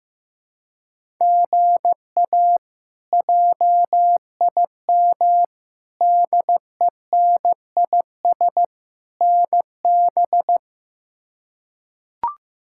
Morse code, heard as a keyed tone, about 15 wpm, 700 Hz. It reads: GA JIM DENIS NB
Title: K1USN SST QSO Practice - 15wpm